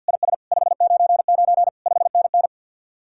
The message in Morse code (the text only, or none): IS 599 5NN